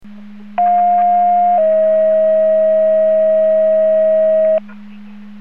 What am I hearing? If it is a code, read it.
Two-tone page, heard as a one-second tone, then a three-second tone, about 710 then 640 Hz. It is heard from 0.6 to 4.6 s.